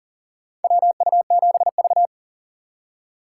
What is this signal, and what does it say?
Morse code, heard as a keyed tone, about 39 wpm, 700 Hz. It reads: WU7V
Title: USA Call Signs - 4 Characters - 6x Character Spacing - 40wpm